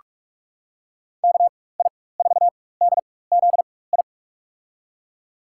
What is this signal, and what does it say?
Morse code, heard as a keyed tone, about 47 wpm, spaced as for 21 wpm, 700 Hz. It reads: KI4DZI
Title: USA Call Signs - 6 Characters - 4x Character Spacing - 45wpm